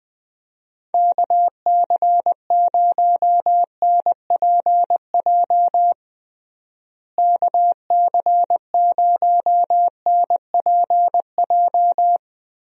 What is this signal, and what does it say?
Morse code, heard as a keyed tone, about 21 wpm, 700 Hz. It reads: KC0NPJ KC0NPJ
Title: USA Call Signs - 6 Characters - Repeated 3x - 20wpm